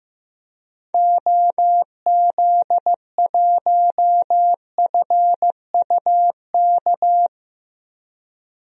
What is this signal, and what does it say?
Morse code, heard as a keyed tone, about 15 wpm, 690 Hz. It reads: OZ1FUK
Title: DXCC-100 Call Signs - 6 Characters - 15wpm